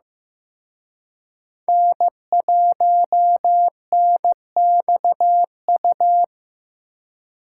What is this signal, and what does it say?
Morse code, heard as a keyed tone, about 18 wpm, 700 Hz. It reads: N1NXU
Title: Speed Racing - USA 1x3 Call Signs 15wpm